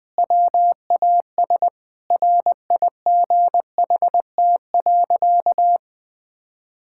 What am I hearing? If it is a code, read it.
Morse code, heard as a keyed tone, about 20 wpm, 700 Hz. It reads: WAS RIGHT.
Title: Sentences with 4-Word Phrases - 20wpm